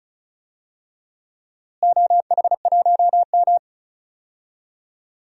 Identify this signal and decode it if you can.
Morse code, heard as a keyed tone, about 36 wpm, 690 Hz. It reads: OH1M
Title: DXCC-100 Call Signs - 4 Characters - Repeated 2x - 35wpm